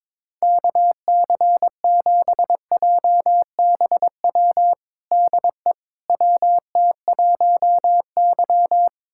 Morse code, heard as a keyed tone, about 21 wpm, 700 Hz. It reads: KC7JBW DE WT1Y
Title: USA Callsign DE Callsign 22wpm